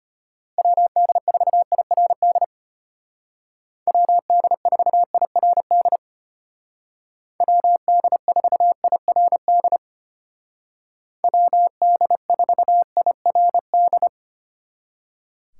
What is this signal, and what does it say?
Morse code, heard as a keyed tone, about 33 wpm, 700 Hz. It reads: WD4IRD WD4IRD WD4IRD WD4IRD